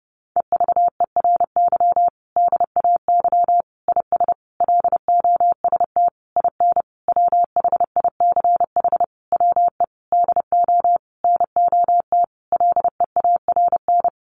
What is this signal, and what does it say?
Morse code, heard as a keyed tone, about 30 wpm, 700 Hz. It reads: EVERY DAY IS LOST IN WHICH WE DO NOT LEARN
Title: Sentences from Top 1400 Words 30wpm